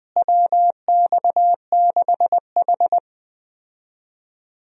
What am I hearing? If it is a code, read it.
Morse code, heard as a keyed tone, about 20 wpm, 700 Hz. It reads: WX6H
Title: USA Call Signs - 4 Characters - Repeated 2x - 20wpm